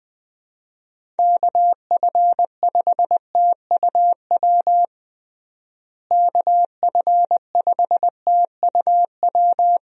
Morse code, heard as a keyed tone, about 20 wpm, 700 Hz. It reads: KF5TUW KF5TUW
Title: USA Call Signs - 6 Characters - Repeated 3x - 20wpm